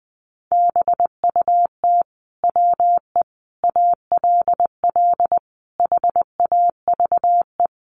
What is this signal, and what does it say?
Morse code, heard as a keyed tone, about 20 wpm, 700 Hz. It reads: BUT WE ALL HAVE
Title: Sentences from Top 1300 Words 20wpm